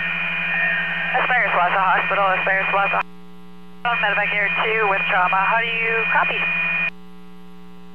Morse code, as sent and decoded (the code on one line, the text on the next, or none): none